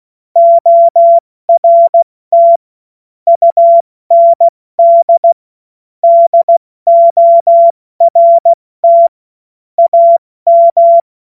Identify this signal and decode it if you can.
Morse code, heard as a keyed tone, about 16 wpm, 680 Hz. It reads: ORT UND DORT AM
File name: Morsekurs / Witze / Witz 20. September 2022 70 F80